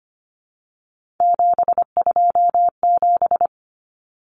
Morse code, heard as a keyed tone, about 25 wpm, 700 Hz. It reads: 727
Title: Numbers - 3-Digit 25wpm